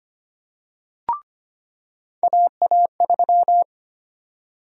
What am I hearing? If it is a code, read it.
Morse code, heard as a keyed tone, about 25 wpm, 700 Hz. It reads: AA3
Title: USA Call Sign Prefixes - 25wpm